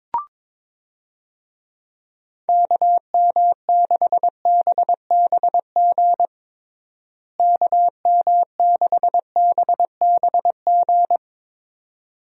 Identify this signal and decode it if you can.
Morse code, heard as a keyed tone, about 22 wpm, 700 Hz. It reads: KM6BBG KM6BBG
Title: USA Call Signs - 6 Characters - Repeated 2x - 22wpm